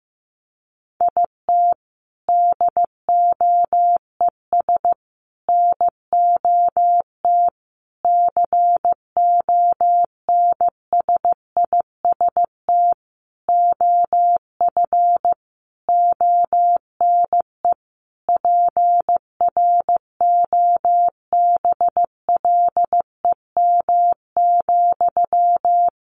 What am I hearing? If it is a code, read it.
Morse code, heard as a keyed tone, about 15 wpm, 700 Hz. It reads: IT DOES NOT CONSIST OF ONE PROBLEM,
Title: Sentences from Top 1400 Words 15wpm